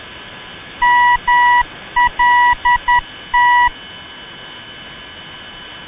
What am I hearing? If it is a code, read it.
Morse code, heard as a keyed tone, about 10 wpm, 970 Hz. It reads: MLT
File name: ndb malta mlt-395